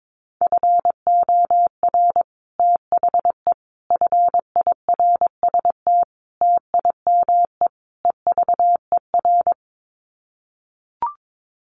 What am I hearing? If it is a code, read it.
Morse code, heard as a keyed tone, about 22 wpm, 700 Hz. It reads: FOR THE FIRST TIME EVER
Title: Sets of 5 Words Encore2 22wpm